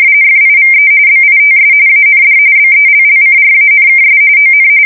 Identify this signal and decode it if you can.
RTTY, about 45.45 baud. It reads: FIVE BITS AND BELLCRANKS ON
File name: fsk170 / USS-Missouri